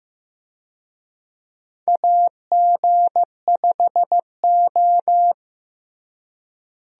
Morse code, heard as a keyed tone, about 15 wpm, 700 Hz. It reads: AG5O
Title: USA Call Signs - 4 Characters - Repeated 2x - 15wpm